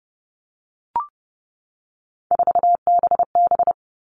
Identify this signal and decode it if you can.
Morse code, heard as a keyed tone, about 30 wpm, 700 Hz. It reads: 4BB
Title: All Cut Numbers - 3-Digit 30wpm